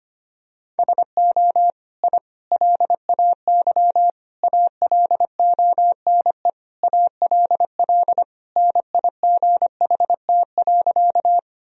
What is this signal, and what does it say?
Morse code, heard as a keyed tone, about 25 wpm, 700 Hz. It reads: SO I LAY ALONE ALL NIGHT.
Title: Sentences from Top 1000 Words - Review Part 2 - 25wpm